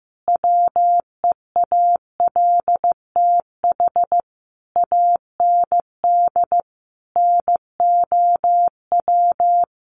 Morse code, heard as a keyed tone, about 15 wpm, 700 Hz. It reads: WEALTH AND NOW
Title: Sentences from Top 1200 Words 15wpm